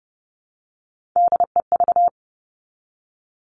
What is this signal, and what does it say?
Morse code, heard as a keyed tone, about 30 wpm, 700 Hz. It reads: DEV
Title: Three Challenging Letter-Number Combinations 30wpm